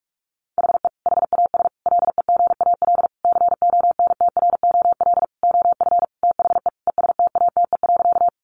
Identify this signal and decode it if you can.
Morse code, heard as a keyed tone, about 44 wpm, 700 Hz. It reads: HE HAS LEGAL CONTROL OF THE ESTATE.
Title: Sentences from Top 1200 Words 45wpm